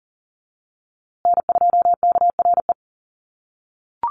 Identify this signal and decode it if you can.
Morse code, heard as a keyed tone, about 40 wpm, 700 Hz. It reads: N2KRE
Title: USA Call Signs Repeated 2x 40wpm